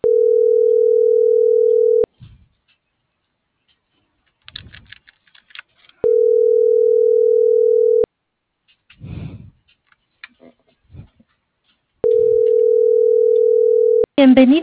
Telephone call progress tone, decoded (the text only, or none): ringback tone